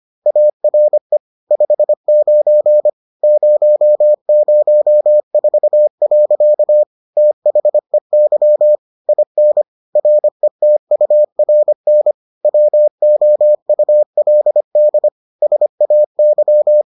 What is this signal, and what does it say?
Morse code, heard as a keyed tone, about 25 wpm, 600 Hz. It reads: ARE 59 004. THEY IN RETURN WOULD SAY